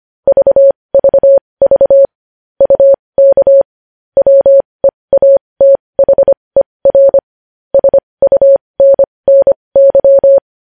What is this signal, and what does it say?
Morse code, heard as a keyed tone, about 25 wpm, spaced as for 20 wpm, 550 Hz. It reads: VVV UK WEATHER SUNNY
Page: news-headlines-in-morse-code-at-20-wpm